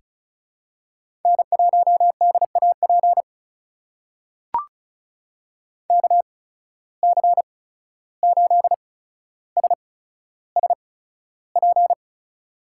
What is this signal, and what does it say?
Morse code, heard as a keyed tone, about 36 wpm, spaced as for 14 wpm, 700 Hz. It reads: N1DAP KC8SSP
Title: USA Call Signs - 8x Character Spacing - 35wpm